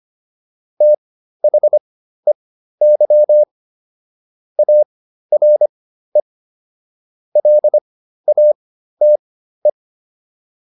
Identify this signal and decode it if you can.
Morse code, heard as a keyed tone, about 25 wpm, spaced as for 13 wpm, 600 Hz. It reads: THEY ARE LATE